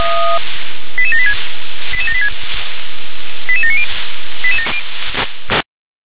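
Five-tone selective call (ZVEI), heard as quick runs of five tones, tones of about 80 ms. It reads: 88786 88786 88790
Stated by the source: ZVEI five-tone selective call received on an SDR.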